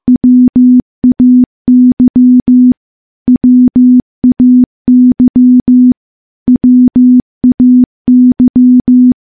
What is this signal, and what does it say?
Morse code, heard as a keyed tone, about 15 wpm, 250 Hz. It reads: WAY WAY WAY
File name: morse / morse-training-15WPM